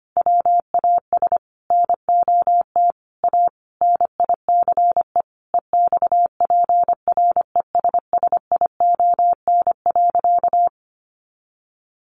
Morse code, heard as a keyed tone, about 25 wpm, 700 Hz. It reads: WAS NOT A NICE EXPRESSION.
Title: Sentences from Top 1200 Words 25wpm